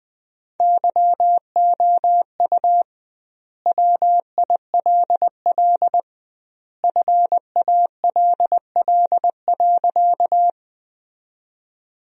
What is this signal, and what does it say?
Morse code, heard as a keyed tone, about 19 wpm, 700 Hz. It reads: YOU WILL FALL.
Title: 2x Word-Spacing - Sentences from Top 500 Words - 20wpm